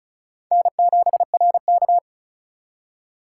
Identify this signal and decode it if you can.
Morse code, heard as a keyed tone, about 44 wpm, 700 Hz. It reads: N7RK
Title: Speed Racing - USA Call Signs 35wpm